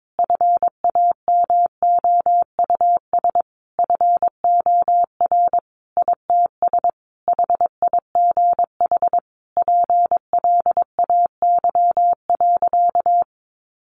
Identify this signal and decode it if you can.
Morse code, heard as a keyed tone, about 22 wpm, 700 Hz. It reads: FAMOUS FOR ITS HIGH PLAY.
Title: Sentences from Top 1500 Words 22wpm